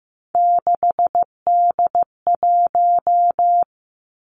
Morse code, heard as a keyed tone, about 15 wpm, 700 Hz. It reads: 6D1
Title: Three Challenging Letter-Number Combinations 15wpm